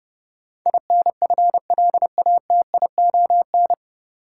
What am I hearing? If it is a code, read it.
Morse code, heard as a keyed tone, about 30 wpm, 700 Hz. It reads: INFLATION
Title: Sentences from Top 1100 Words - New Word Drill Alt - 30wpm